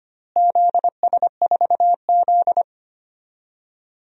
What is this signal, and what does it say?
Morse code, heard as a keyed tone, about 25 wpm, 700 Hz. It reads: ZS4Z
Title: DXCC-100 Call Signs - 4 Characters - Repeated 3x - 25wpm